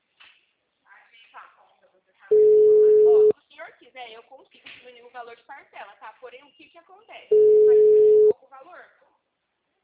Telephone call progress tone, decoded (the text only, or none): ringback tone